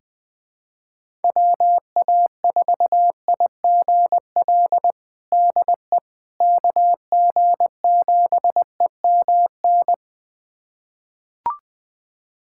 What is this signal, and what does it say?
Morse code, heard as a keyed tone, about 21 wpm, 700 Hz. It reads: WA4IGL DE KG7EMN
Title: USA Callsign DE Callsign - 6 Characters - 20wpm